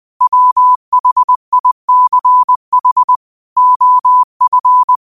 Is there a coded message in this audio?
Yes: Morse code from 0.2 to 5.0 s, heard as a keyed tone, about 20 wpm, 1000 Hz. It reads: WHICH OF